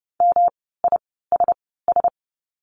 Morse code, heard as a keyed tone, about 30 wpm, spaced as for 17 wpm, 700 Hz. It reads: MISS
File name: course / 30:10wpm / 108 Lesson 18f - Letter m - Words 30wpm